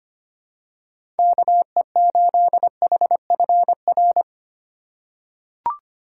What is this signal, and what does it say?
Morse code, heard as a keyed tone, about 25 wpm, 700 Hz. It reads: KE8HFR
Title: USA Call Signs - 6 Characters - 2x Character Spacing - 25wpm